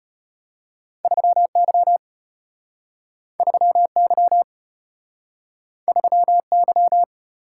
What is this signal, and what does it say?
Morse code, heard as a keyed tone, about 34 wpm, 700 Hz. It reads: 3Y 3Y 3Y